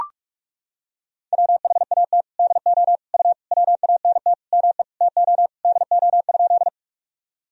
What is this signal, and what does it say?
Morse code, heard as a keyed tone, about 43 wpm, 700 Hz. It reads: WHAT DO U WANT ME TO DO?
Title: Sentences from Top 200 Words and Abbreviations 45wpm